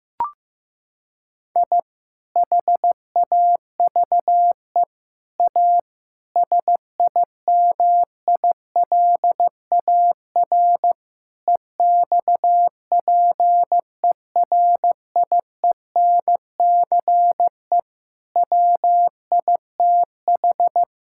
Morse code, heard as a keyed tone, about 15 wpm, 700 Hz. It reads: I HAVE A SIMILAR EXPERIENCE WITH
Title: Sentences from Top 1000 Words - Review Part 5 - 15wpm